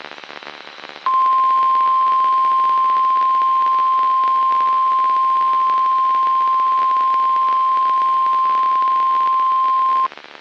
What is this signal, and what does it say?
Emergency Alert System (SAME) tones (the 1050 Hz attention tone). It is heard from 1.0 to 10.1 s.
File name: eas messages / EAS WXM95 Towanda